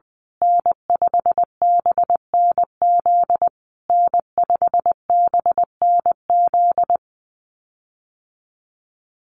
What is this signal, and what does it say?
Morse code, heard as a keyed tone, about 20 wpm, 700 Hz. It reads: N5BNZ N5BNZ
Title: USA Call Signs Repeated 3x 20wpm